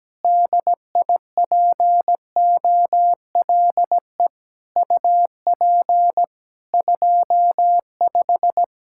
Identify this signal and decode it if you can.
Morse code, heard as a keyed tone, about 17 wpm, 700 Hz. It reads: DIPOLE UP 25
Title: Practice QSO - Exchange 2 - Info - 17wpm